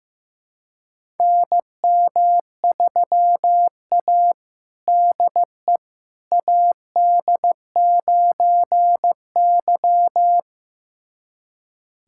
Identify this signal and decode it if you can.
Morse code, heard as a keyed tone, about 15 wpm, 700 Hz. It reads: NM3A DE AD9Y
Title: USA Callsign DE Callsign - 4 Characters - 15wpm